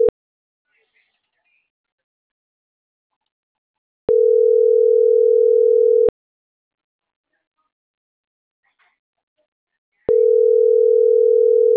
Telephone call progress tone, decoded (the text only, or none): ringback tone